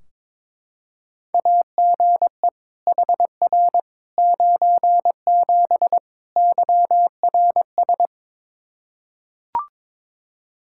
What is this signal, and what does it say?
Morse code, heard as a keyed tone, about 21 wpm, 700 Hz. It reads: AGE HR 97 YRS
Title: Practice QSO - Exchange 3 - More Info - 22wpm